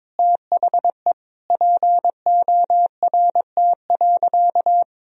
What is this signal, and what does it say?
Morse code, heard as a keyed tone, about 21 wpm, 700 Hz. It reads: THE PORT.
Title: Sentences from Top 1000 Words - Review Part 1 - 22wpm